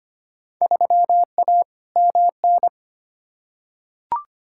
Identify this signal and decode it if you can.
Morse code, heard as a keyed tone, about 25 wpm, 700 Hz. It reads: 3A MN
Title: Field Day Exchanges - 25wpm